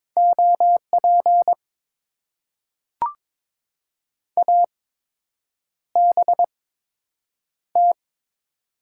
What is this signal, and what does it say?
Morse code, heard as a keyed tone, about 22 wpm, spaced as for 8 wpm, 700 Hz. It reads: OP ABT
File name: Top 90 QSO Elements - 8x Character Spacing - 22wpm